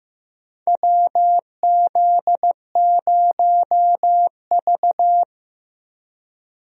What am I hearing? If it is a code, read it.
Morse code, heard as a keyed tone, about 15 wpm, 700 Hz. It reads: WZ0V
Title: USA Call Signs - 4 Characters - Repeated 3x - 15wpm